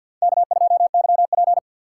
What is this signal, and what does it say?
Morse code, heard as a keyed tone, about 51 wpm, 700 Hz. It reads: K2YP
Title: USA Call Signs - 4 Characters - 6x Character Spacing - 50wpm